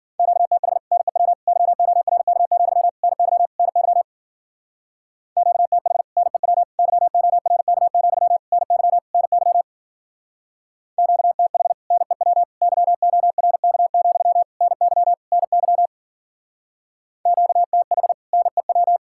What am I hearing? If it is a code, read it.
Morse code, heard as a keyed tone, about 50 wpm, 700 Hz. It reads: QTH NEW YORK, NY NY QTH NEW YORK, NY NY QTH NEW YORK, NY NY QTH NEW
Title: Speed-Racing Common QSO Phrases Encore - 40wpm